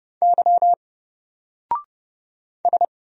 Morse code, heard as a keyed tone, about 29 wpm, 700 Hz. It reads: Y S
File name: Single-Letter-Number-30wpm